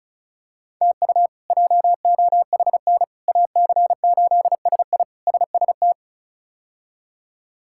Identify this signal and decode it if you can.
Morse code, heard as a keyed tone, about 36 wpm, 700 Hz. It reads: TU JOHN AC8SI SST
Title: K1USN SST QSO Practice - 35wpm